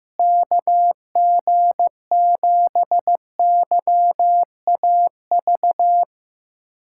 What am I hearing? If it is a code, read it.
Morse code, heard as a keyed tone, about 15 wpm, 700 Hz. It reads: KG7YAV
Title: USA Callsign DE Callsign 15wpm